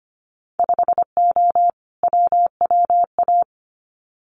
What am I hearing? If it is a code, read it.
Morse code, heard as a keyed tone, about 25 wpm, 690 Hz. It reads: 5O WWA
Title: Winter Field Day Exchanges Encore 25wpm